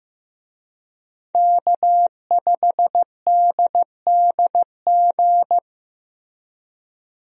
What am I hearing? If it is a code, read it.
Morse code, heard as a keyed tone, about 15 wpm, 700 Hz. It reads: K5DDG
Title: USA Call Signs - 5 Characters - Repeated 2x - 15wpm